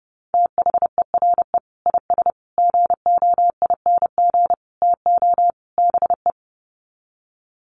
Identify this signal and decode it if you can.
Morse code, heard as a keyed tone, about 30 wpm, 700 Hz. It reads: THERE IS GOING TO BE
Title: Sets of 5 Words Encore 30wpm